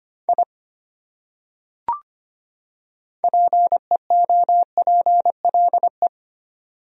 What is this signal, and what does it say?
Morse code, heard as a keyed tone, about 25 wpm, 700 Hz. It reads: I PEOPLE
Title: Top 100 Words 25wpm